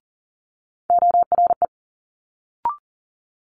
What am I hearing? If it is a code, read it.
Morse code, heard as a keyed tone, about 40 wpm, 700 Hz. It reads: ORE
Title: Multi-letter Phonemes Quick Fire 40wpm